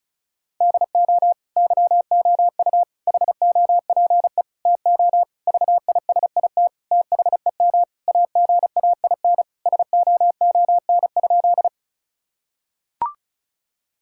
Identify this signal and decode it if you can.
Morse code, heard as a keyed tone, about 36 wpm, 700 Hz. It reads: DO YOU HOPE TO VISIT THEM AGAIN SOON?
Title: Sentences from Top 1000 Words - Review Part 6 - 35wpm